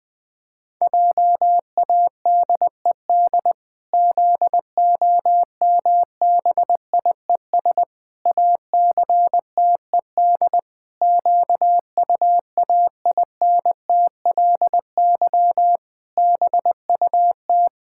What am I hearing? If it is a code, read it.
Morse code, heard as a keyed tone, about 20 wpm, 700 Hz. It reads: JADED ZOMBIES ACTED QUAINTLY BUT
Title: Pangram Challenge v2 - 20wpm